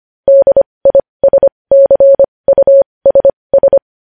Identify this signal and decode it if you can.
Morse code, heard as a keyed tone, about 25 wpm, spaced as for 20 wpm, 550 Hz. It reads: DISCUSS
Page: news headlines in morse code at 20 wpm